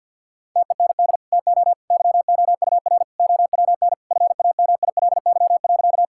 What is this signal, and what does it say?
Morse code, heard as a keyed tone, about 50 wpm, 700 Hz. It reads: TEND TO YOUR OWN FAMILY.
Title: Sentences from Top 1000 Words - Review Part 5 - 50wpm